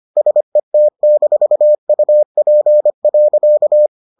Morse code, heard as a keyed tone, about 25 wpm, 600 Hz. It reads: SET-UP.